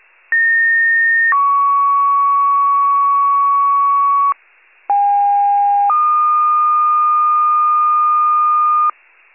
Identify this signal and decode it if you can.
Two-tone page, heard as a one-second tone, then a three-second tone, about 1850 then 1120 Hz, about 810 then 1220 Hz. It is heard from 0.3 to 8.9 s.
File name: pager / 73 RESCUE / wcalert